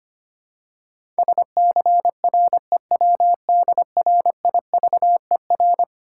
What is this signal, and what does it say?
Morse code, heard as a keyed tone, about 25 wpm, 700 Hz. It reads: SCREWDRIVER
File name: practice / 25wpm / Easy Words with Hints - 25wpm